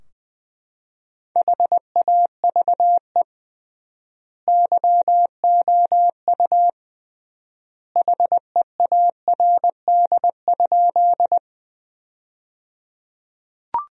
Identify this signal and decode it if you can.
Morse code, heard as a keyed tone, about 21 wpm, 700 Hz. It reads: HAVE YOU HEARD?
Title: 3x Word-Spacing - Sentences from Top 400 Words - 20wpm